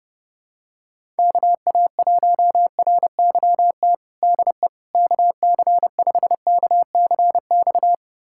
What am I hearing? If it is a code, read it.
Morse code, heard as a keyed tone, about 32 wpm, 700 Hz. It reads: KA1RYT DE KC5KCX
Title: USA Callsign DE Callsign - 6 Characters - 30wpm